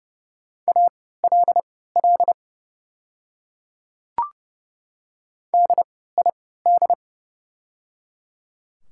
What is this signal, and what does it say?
Morse code, heard as a keyed tone, about 30 wpm, spaced as for 17 wpm, 700 Hz. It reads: ALL DID
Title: 3x Character Spacing - Top 100 Words - 30wpm